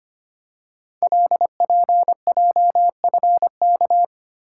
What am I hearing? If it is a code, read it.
Morse code, heard as a keyed tone, about 25 wpm, 700 Hz. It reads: LPJFK
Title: Code Groups - 5 Letters - 25wpm